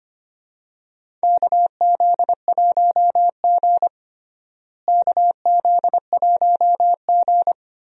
Morse code, heard as a keyed tone, about 25 wpm, 700 Hz. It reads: KZ1G KZ1G
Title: USA Call Signs - 4 Characters - Repeated 3x - 25wpm